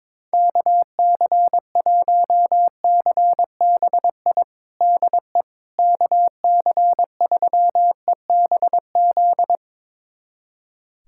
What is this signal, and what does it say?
Morse code, heard as a keyed tone, about 22 wpm, 700 Hz. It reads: KC1CBI DE KC3EBZ
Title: USA Callsign DE Callsign - 6 Characters - 22wpm